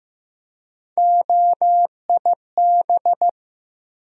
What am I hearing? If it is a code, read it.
Morse code, heard as a keyed tone, about 15 wpm, 700 Hz. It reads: OIB
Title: USA Call Sign Postfixes - 3 Letters 15wpm